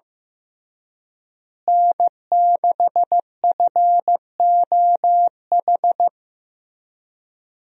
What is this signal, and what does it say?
Morse code, heard as a keyed tone, about 15 wpm, 700 Hz. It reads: N6FOH
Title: USA Call Signs - 5 Characters - Repeated 3x - 15wpm